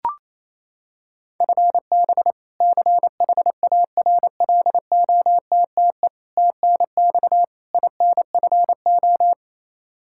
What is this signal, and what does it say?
Morse code, heard as a keyed tone, about 28 wpm, 700 Hz. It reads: FB CHARLOTTE TNX INFO
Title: Practice QSO - Exchange 3 - More Info - 28wpm